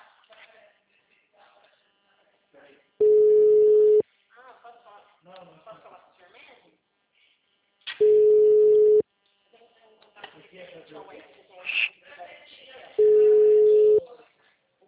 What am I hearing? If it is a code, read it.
Ringback tone (Europe, 425 Hz), with speech.